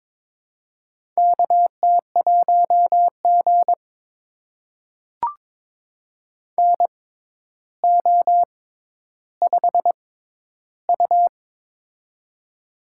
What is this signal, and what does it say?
Morse code, heard as a keyed tone, about 23 wpm, spaced as for 11 wpm, 700 Hz. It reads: KT1G NO5U
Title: USA Call Signs - 4 Characters - 6x Character Spacing - 22wpm